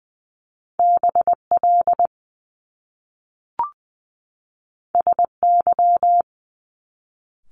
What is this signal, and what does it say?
Morse code, heard as a keyed tone, about 20 wpm, 700 Hz. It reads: BL SY
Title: Top Two Letter Combinations in Words 20wpm